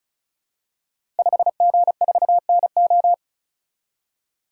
Morse code, heard as a keyed tone, about 36 wpm, 700 Hz. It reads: 5G4NO
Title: Code Groups - 5 Letter:Numbers - 35wpm